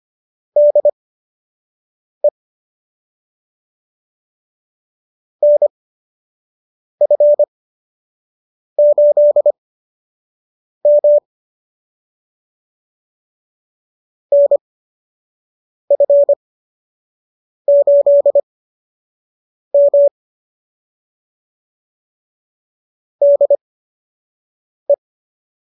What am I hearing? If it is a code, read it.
Morse code, heard as a keyed tone, about 25 wpm, spaced as for 6 wpm, 590 Hz. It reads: DE NF8M NF8M DE